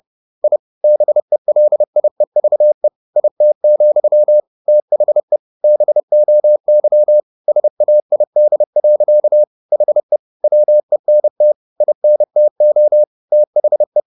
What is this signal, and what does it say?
Morse code, heard as a keyed tone, about 30 wpm, 600 Hz. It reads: I BELIEVE IT, THE BOY SAID. HE WENT INTO THE